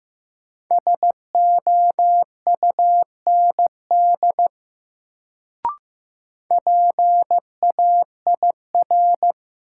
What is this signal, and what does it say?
Morse code, heard as a keyed tone, about 15 wpm, 700 Hz. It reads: SOUND PAIR
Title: Sets of 3 Words - New Word Drill - 15wpm